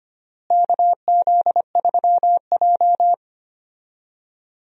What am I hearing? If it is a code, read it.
Morse code, heard as a keyed tone, about 25 wpm, 700 Hz. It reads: KZ3J
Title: USA Call Signs - 4 Characters - Repeated 3x - 25wpm